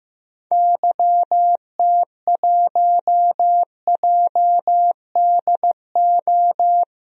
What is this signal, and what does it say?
Morse code, heard as a keyed tone, about 15 wpm, 700 Hz. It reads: YT1JDO
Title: DXCC-100 Call Signs - 6 Characters - 15wpm